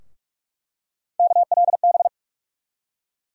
Morse code, heard as a keyed tone, about 46 wpm, 700 Hz. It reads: KLB
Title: USA Call Sign Postfixes - 3 Letters 45wpm